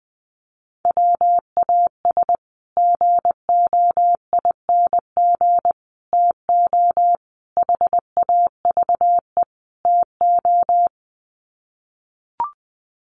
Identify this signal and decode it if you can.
Morse code, heard as a keyed tone, about 20 wpm, 700 Hz. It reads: WAS GOING TO HAVE TO
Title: Sets of 5 Words Encore 20wpm